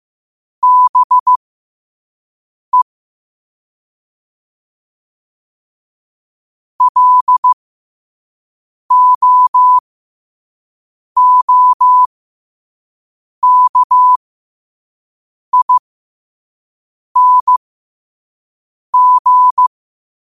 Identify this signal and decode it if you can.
Morse code, heard as a keyed tone, about 15 wpm, spaced as for 5 wpm, 1000 Hz. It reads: BE LOOKING
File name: qotdfiles / 5wpm